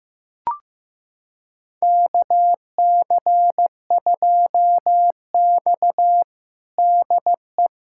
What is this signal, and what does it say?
Morse code, heard as a keyed tone, about 15 wpm, 700 Hz. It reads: KC2X DE
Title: USA Callsign DE Callsign - 4 Characters - 15wpm